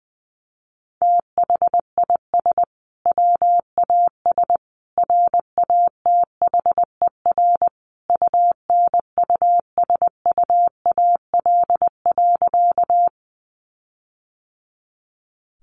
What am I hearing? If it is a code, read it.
Morse code, heard as a keyed tone, about 20 wpm, 700 Hz. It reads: THIS WAS RATHER UNUSUAL.
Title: Sentences from Top 1400 Words 20wpm